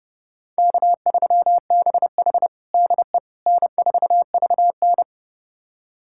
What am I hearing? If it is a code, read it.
Morse code, heard as a keyed tone, about 32 wpm, 700 Hz. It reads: K3BH DE N4VN
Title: USA Callsign DE Callsign - 4 Characters - 30wpm